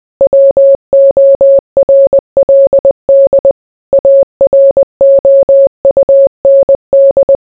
Morse code, heard as a keyed tone, about 20 wpm, 550 Hz. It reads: WORLD AROUND